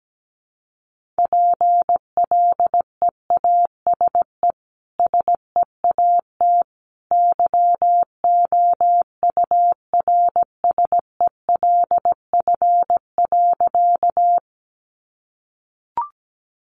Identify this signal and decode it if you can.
Morse code, heard as a keyed tone, about 17 wpm, 700 Hz. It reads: PLEASE SEAT YOURSELF.
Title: Sentences from Top 1500 Words 17wpm